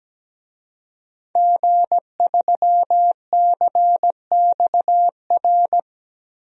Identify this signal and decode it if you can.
Morse code, heard as a keyed tone, about 17 wpm, 700 Hz. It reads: G3CXR
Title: DXCC-100 Call Signs - 5 Characters - 17wpm